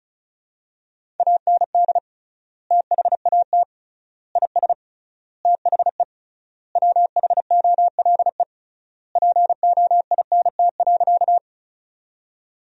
Morse code, heard as a keyed tone, about 36 wpm, 700 Hz. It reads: AND THAT IS THE WHOLE POINT.
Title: 3x Word-Spacing - Sentences from Top 400 Words - 35wpm